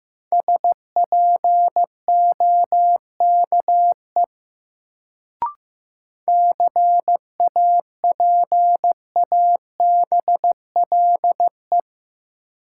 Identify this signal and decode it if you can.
Morse code, heard as a keyed tone, about 15 wpm, 700 Hz. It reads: SPOKE CAPABLE
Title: Sentences from Top 1200 Words - New Word Drill Alt - 15wpm